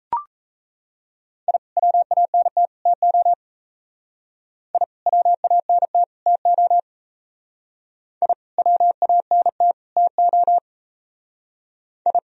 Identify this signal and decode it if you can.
Morse code, heard as a keyed tone, about 35 wpm, 700 Hz. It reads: I WANT TO I WANT TO I WANT TO I
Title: Speed-Racing - Sets of 3 Words 28wpm